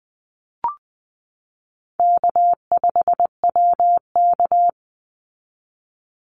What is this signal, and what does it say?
Morse code, heard as a keyed tone, about 20 wpm, 700 Hz. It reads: K5WK
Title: Easy Call Signs Repeated 3x - 20wpm